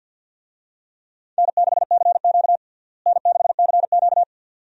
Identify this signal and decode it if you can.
Morse code, heard as a keyed tone, about 51 wpm, 700 Hz. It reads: N6CQ N6CQ
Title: USA Call Signs - 4 Characters - Repeated 2x - 50wpm